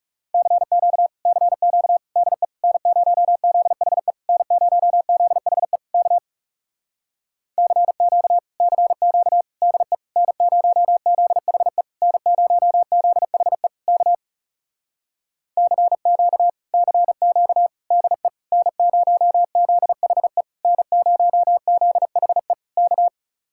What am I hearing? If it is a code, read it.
Morse code, heard as a keyed tone, about 40 wpm, 700 Hz. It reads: CQ CQ DE N0ZHE N0ZHE K CQ CQ DE N0ZHE N0ZHE K CQ CQ DE N0ZHE N0ZHE K